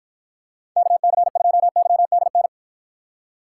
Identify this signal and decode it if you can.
Morse code, heard as a keyed tone, about 52 wpm, 700 Hz. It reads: KC2YDN